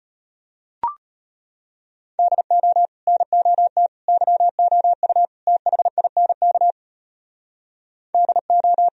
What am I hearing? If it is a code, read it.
Morse code, heard as a keyed tone, about 31 wpm, 700 Hz. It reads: DO NOT YOU THINK DO
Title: Speed-Racing - Sets of 4 Words 25wpm